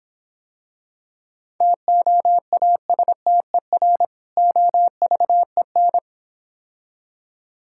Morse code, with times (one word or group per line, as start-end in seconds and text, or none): 1.6-4.0 TOASTER
4.4-6.0 OVEN